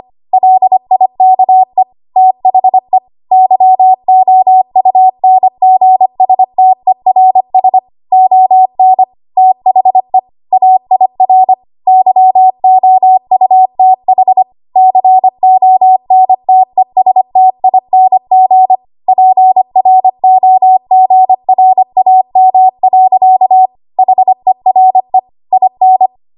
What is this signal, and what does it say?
Morse code, heard as a keyed tone, about 25 wpm, 750 Hz. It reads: LIKE THE YOUNGSTERS ON THE AIR YOUTH CONTESTING PROGRAM. HERE IN
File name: Archive / 25 WPM / 25WPM